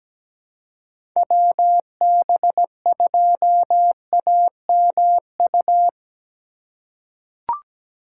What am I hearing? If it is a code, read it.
Morse code, heard as a keyed tone, about 17 wpm, 700 Hz. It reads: WB2AMU
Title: USA Call Signs - 6 Characters - 3x Character Spacing - 17wpm